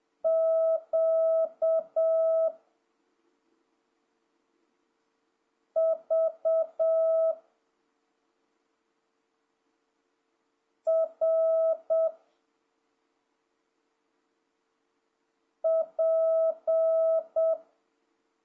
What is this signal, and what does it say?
Morse code, heard as a keyed tone, about 7 wpm, 650 Hz. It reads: Q V R P
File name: jh8cbh / 2026cw